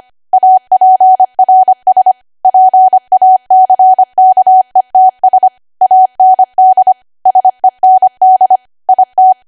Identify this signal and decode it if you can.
Morse code, heard as a keyed tone, about 25 wpm, 750 Hz. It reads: APRS PACKETS AND SEND IT
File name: Archive / 25 WPM / 25WPM